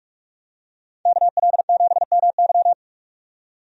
Morse code, heard as a keyed tone, about 45 wpm, 700 Hz. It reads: KL7MY
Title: USA Call Signs - 5 Characters - 45wpm